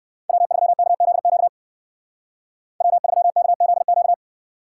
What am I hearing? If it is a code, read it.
Morse code, heard as a keyed tone, about 54 wpm, 700 Hz. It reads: W3KZQ W3KZQ
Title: Speed Racing - USA 1x3 Call Signs 45wpm